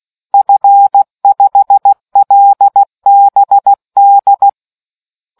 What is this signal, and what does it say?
Morse code, heard as a keyed tone, about 16 wpm, 800 Hz. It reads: F5LBD